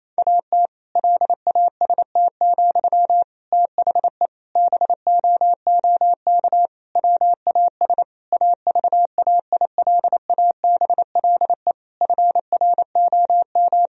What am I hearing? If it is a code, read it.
Morse code, heard as a keyed tone, about 28 wpm, 700 Hz. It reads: AT LAST, THE BOOK WAS AVAILABLE FROM
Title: Sentences from Top 1000 Words - Review Part 3 - 28wpm